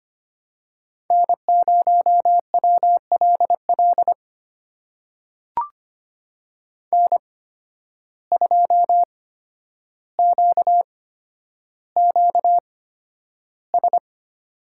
Morse code, heard as a keyed tone, about 25 wpm, spaced as for 11 wpm, 700 Hz. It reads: N0WLL N2QQS
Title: USA Call Signs - 5 Characters - 8x Character Spacing - 25wpm